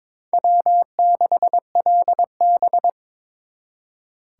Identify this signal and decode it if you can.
Morse code, heard as a keyed tone, about 26 wpm, 700 Hz. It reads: W6LB
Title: Speed Racing - USA 1x2 Call Signs 22wpm